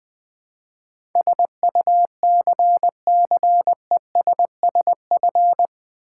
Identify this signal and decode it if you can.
Morse code, heard as a keyed tone, about 20 wpm, 700 Hz. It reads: SUCCESSF
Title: Build A Word - 4 Letters At A Time - 10-Letter Words - 20wpm